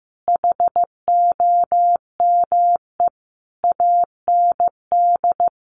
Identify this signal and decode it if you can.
Morse code, heard as a keyed tone, about 15 wpm, 700 Hz. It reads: HOME AND
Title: Additional Easy Phrases 15wpm